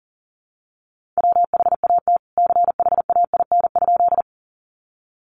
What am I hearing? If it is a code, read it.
Morse code, heard as a keyed tone, about 40 wpm, 700 Hz. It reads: WHAT CHAIN?
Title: Sentences from Top 1500 Words 40wpm